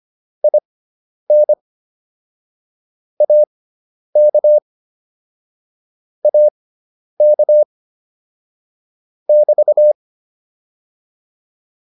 Morse code, heard as a keyed tone, about 25 wpm, spaced as for 10 wpm, 600 Hz. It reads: IN AK AK <BT>
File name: QSO107 10